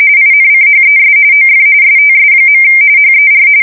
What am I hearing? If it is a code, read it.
RTTY, about 45.45 baud. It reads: INCHES OF STS. HER
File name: fsk170 / battleshipnj